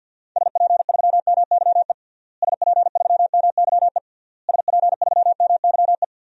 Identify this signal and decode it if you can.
Morse code, heard as a keyed tone, about 53 wpm, 700 Hz. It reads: SP3MYE SP3MYE SP3MYE
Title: DXCC-100 Call Signs - 6 Characters - Repeated 3x - 50wpm